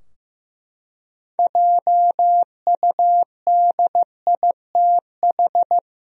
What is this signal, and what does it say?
Morse code, heard as a keyed tone, about 15 wpm, 700 Hz. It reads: JUDITH
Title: USA Common Names - 15wpm